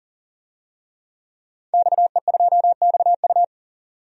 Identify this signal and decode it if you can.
Morse code, heard as a keyed tone, about 40 wpm, 700 Hz. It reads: XE2XU
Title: DXCC-100 Call Signs - 5 Characters - 40wpm